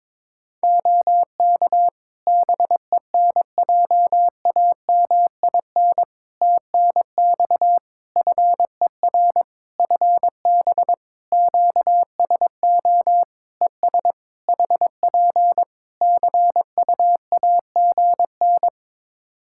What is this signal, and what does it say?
Morse code, heard as a keyed tone, about 22 wpm, 700 Hz. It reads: OK BENJAMIN TNX FER FB QSO ES HP CUAGN
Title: Practice QSO - Exchange 4 - Ending - 22wpm